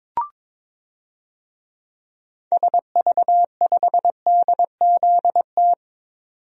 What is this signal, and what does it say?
Morse code, heard as a keyed tone, about 22 wpm, 700 Hz. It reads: SV5DZT
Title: DXCC-100 Call Signs - 6 Characters - 22wpm